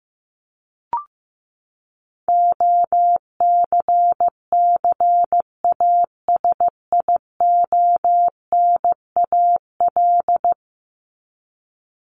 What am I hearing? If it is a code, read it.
Morse code, heard as a keyed tone, about 15 wpm, 700 Hz. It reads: OCCASIONAL
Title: LY Suffix - Top 100 Words with Base Words 15wpm